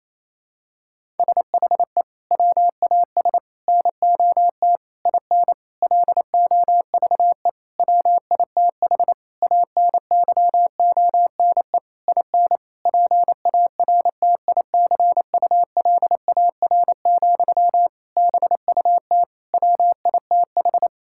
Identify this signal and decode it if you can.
Morse code, heard as a keyed tone, about 28 wpm, 700 Hz. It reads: SHE WAS NOT IN LOVE WITH ANYONE IN PARTICULAR, BUT WITH
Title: Sentences from Top 1000 Words - Review Part 5 - 28wpm